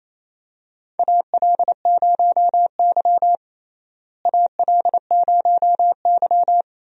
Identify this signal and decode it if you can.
Morse code, heard as a keyed tone, about 28 wpm, 700 Hz. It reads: AL0Y AL0Y
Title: USA Call Signs - 4 Characters - Repeated 3x - 28wpm